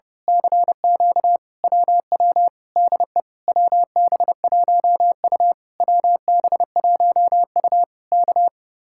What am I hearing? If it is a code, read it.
Morse code, heard as a keyed tone, about 30 wpm, 700 Hz. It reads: CQ WW DE WB1U WB1U K
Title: Speed-Racing Common QSO Phrases - 30wpm